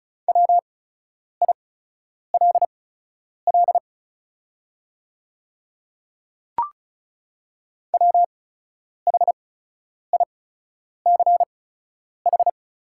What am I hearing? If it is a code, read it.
Morse code, heard as a keyed tone, about 36 wpm, spaced as for 10 wpm, 700 Hz. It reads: WILL WHICH
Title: 8x Character Spacing - Top 500 Words - 35wpm